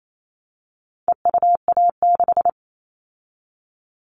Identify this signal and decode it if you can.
Morse code, heard as a keyed tone, about 28 wpm, 700 Hz. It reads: EUA6
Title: All Cut Numbers - 4-Digit 28wpm